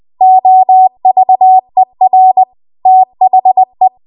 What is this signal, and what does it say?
Morse code, heard as a keyed tone, about 20 wpm, 750 Hz. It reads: OVER THE